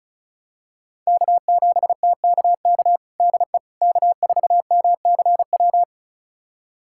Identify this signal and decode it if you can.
Morse code, heard as a keyed tone, about 36 wpm, 700 Hz. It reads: K7TKK DE K4MCW
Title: USA Callsign DE Callsign - 5 Characters - 35wpm